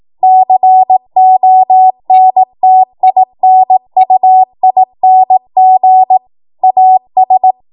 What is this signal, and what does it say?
Morse code, heard as a keyed tone, about 18 wpm, 750 Hz. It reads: CONTINUING AS
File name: Morse / 18WPM